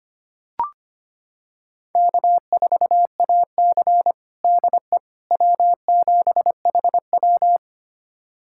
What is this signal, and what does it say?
Morse code, heard as a keyed tone, about 25 wpm, 700 Hz. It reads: K4AC DE W7HW
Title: USA Callsign DE Callsign - 4 Characters - 25wpm